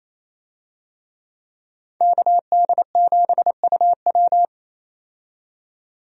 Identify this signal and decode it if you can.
Morse code, heard as a keyed tone, about 28 wpm, 700 Hz. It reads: KD7UW
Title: USA Call Signs - 5 Characters - 28wpm